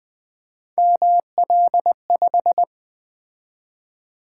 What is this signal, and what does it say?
Morse code, heard as a keyed tone, about 20 wpm, 700 Hz. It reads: ML5
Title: Three Letter-Number Combinations - 20wpm